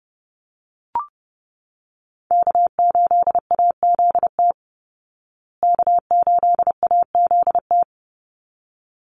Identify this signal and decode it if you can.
Morse code, heard as a keyed tone, about 30 wpm, 700 Hz. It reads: K8AZT K8AZT
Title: Call Signs Repeated 3x - 30wpm